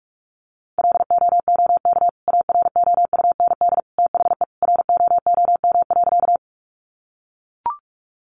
Morse code, heard as a keyed tone, about 44 wpm, 700 Hz. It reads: LOOK AROUND THE ROOM.
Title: Sentences from Top 600 Words 45wpm